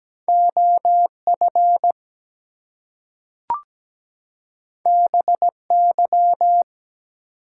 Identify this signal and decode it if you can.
Morse code, heard as a keyed tone, about 17 wpm, 700 Hz. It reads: OF BY
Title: Two Letter Words - 17wpm